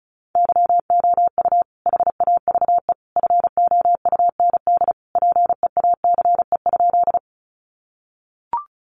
Morse code, heard as a keyed tone, about 34 wpm, 700 Hz. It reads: YOU HAVE FOUND PEACE?
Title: Sentences from Top 1500 Words 35wpm